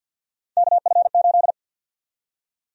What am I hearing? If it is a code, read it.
Morse code, heard as a keyed tone, about 52 wpm, 700 Hz. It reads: KF8
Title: USA Call Sign Prefixes - 50wpm